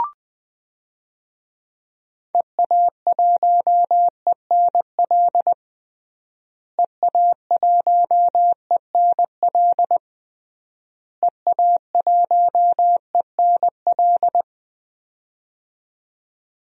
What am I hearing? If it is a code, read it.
Morse code, heard as a keyed tone, about 19 wpm, 700 Hz. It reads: EA1ENL EA1ENL EA1ENL
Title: DXCC-100 Call Signs - 6 Characters - Repeated 3x - 20wpm